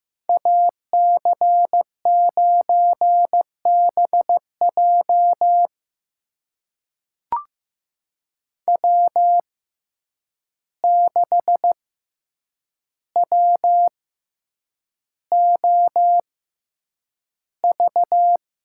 Morse code, heard as a keyed tone, about 15 wpm, spaced as for 8 wpm, 700 Hz. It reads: AC9BJ W6WOV
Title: USA Call Signs - 5 Characters - 6x Character Spacing - 15wpm